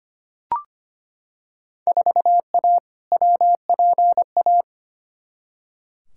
Morse code, heard as a keyed tone, about 25 wpm, 700 Hz. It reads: 4A WPA
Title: Field Day Exchanges Encore - 25wpm